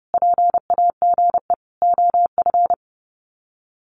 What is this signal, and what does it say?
Morse code, heard as a keyed tone, about 30 wpm, 700 Hz. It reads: PAGE OF
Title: Sets of 5 Words Encore 30wpm